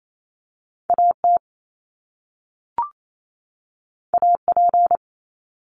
Morse code, heard as a keyed tone, about 28 wpm, 700 Hz. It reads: AT AP
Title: Top Two Letter Combinations in Words 28wpm